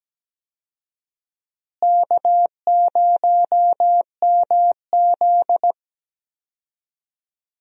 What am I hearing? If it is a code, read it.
Morse code, heard as a keyed tone, about 17 wpm, 700 Hz. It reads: K0MZ
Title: USA Call Signs - 4 Characters - Repeated 3x - 17wpm